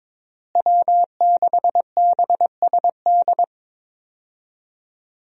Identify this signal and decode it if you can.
Morse code, heard as a keyed tone, about 22 wpm, 700 Hz. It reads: W6BSD
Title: USA Call Signs - 5 Characters - Repeated 3x - 22wpm